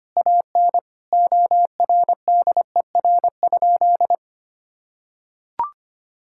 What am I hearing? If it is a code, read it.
Morse code, heard as a keyed tone, about 25 wpm, 700 Hz. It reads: AN ORDER?
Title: Sentences from Top 1600 Words 25wpm